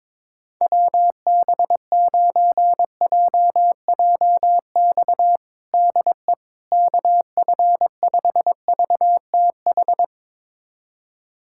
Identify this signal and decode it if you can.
Morse code, heard as a keyed tone, about 22 wpm, 700 Hz. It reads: WB9JJX DE KF5VTH
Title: USA Callsign DE Callsign - 6 Characters - 22wpm